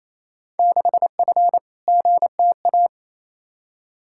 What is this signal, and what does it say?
Morse code, heard as a keyed tone, about 28 wpm, 700 Hz. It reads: 6F GTA
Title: Field Day Exchanges Encore - 28wpm